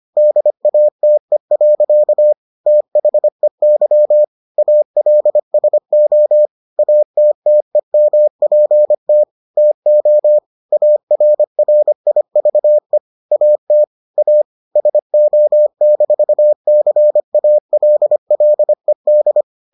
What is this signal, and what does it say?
Morse code, heard as a keyed tone, about 25 wpm, 600 Hz. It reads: DATE. THEY ALSO ATTEMPT TO ARRIVE AT A SO-CALLED